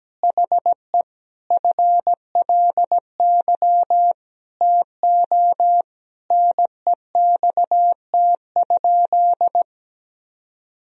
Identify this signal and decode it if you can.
Morse code, heard as a keyed tone, about 17 wpm, 700 Hz. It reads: HE FLY TO NEXT?
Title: Sentences from Top 1000 Words - Review Part 2 - 17wpm